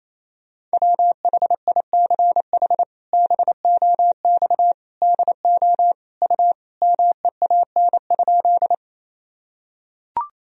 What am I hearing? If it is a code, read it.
Morse code, heard as a keyed tone, about 28 wpm, 700 Hz. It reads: WHICH BOX DO U MEAN?
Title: Sentences from Top 500 Words and Abbreviations 28wpm